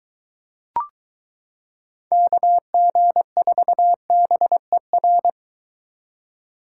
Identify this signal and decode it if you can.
Morse code, heard as a keyed tone, about 22 wpm, 700 Hz. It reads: KG4BER